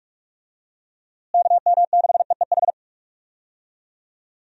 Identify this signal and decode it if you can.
Morse code, heard as a keyed tone, about 41 wpm, 700 Hz. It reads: KM6EEH
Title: USA Call Signs - 6 Characters - Repeated 2x - 45wpm